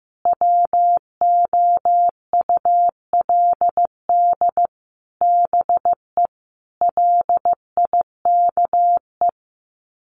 Morse code, heard as a keyed tone, about 15 wpm, 700 Hz. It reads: WOULD BE LIKE
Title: Sets of 5 Words Encore 15wpm